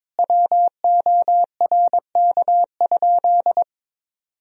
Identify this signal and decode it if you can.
Morse code, heard as a keyed tone, about 22 wpm, 700 Hz. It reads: WORK?
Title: Sentences from Top 1000 Words - Review Part 4 - 22wpm